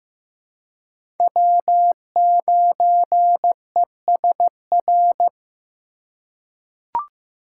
Morse code, heard as a keyed tone, about 15 wpm, 700 Hz. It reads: W9ESR
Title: USA Call Signs - 5 Characters - 2x Character Spacing - 15wpm